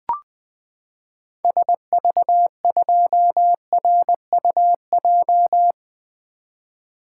Morse code, heard as a keyed tone, about 20 wpm, 700 Hz. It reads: SV2RUJ
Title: 2023 SOTA Activator Call Signs - 20wpm